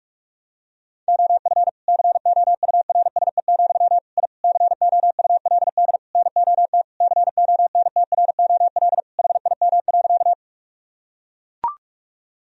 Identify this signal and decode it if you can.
Morse code, heard as a keyed tone, about 41 wpm, 700 Hz. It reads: OF COURSE, I COULD NOT CONTROL HIM.
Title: Sentences from Top 1000 Words - Review Part 3 - 45wpm